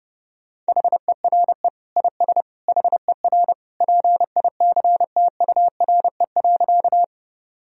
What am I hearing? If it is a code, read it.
Morse code, heard as a keyed tone, about 30 wpm, 700 Hz. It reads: HERE IS HER PICTURE.
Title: Sentences from Top 1000 Words - Review Part 1 - 30wpm